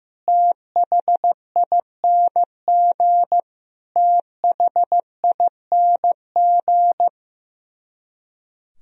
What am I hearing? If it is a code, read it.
Morse code, heard as a keyed tone, about 15 wpm, 700 Hz. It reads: THING THING
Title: Top 100 Words Repeated 15wpm